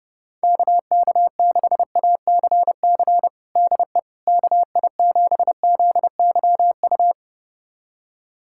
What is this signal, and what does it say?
Morse code, heard as a keyed tone, about 30 wpm, 690 Hz. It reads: KK6ACC DE KI7ZYU
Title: USA Callsign DE Callsign - 6 Characters - 30wpm